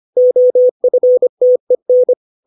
Morse code, heard as a keyed tone, about 25 wpm, 500 Hz. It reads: OFTEN
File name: 632 words / 632 Words 25WPM